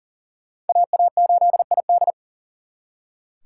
Morse code, heard as a keyed tone, about 52 wpm, 700 Hz. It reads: AA8ID